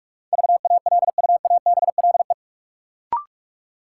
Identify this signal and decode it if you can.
Morse code, heard as a keyed tone, about 46 wpm, 700 Hz. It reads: VALUABLE
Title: Sentences from Top 1100 Words - New Word Drill Alt - 45wpm